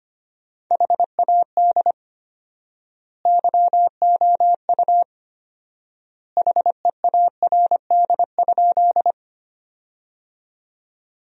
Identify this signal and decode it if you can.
Morse code, heard as a keyed tone, about 25 wpm, 700 Hz. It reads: HAD YOU HEARD?
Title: 4x Word-Spacing - Sentences from Top 400 Words - 25wpm